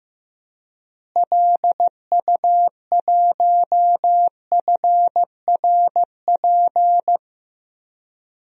Morse code, heard as a keyed tone, about 15 wpm, 700 Hz. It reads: LU1FRP
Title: DXCC-100 Call Signs - 6 Characters - Repeated 2x - 15wpm